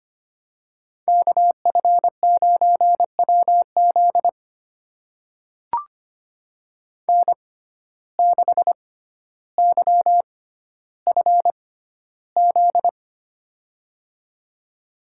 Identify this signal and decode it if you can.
Morse code, heard as a keyed tone, about 25 wpm, spaced as for 13 wpm, 700 Hz. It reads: KF9WZ N6YFZ
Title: USA Call Signs - 5 Characters - 6x Character Spacing - 25wpm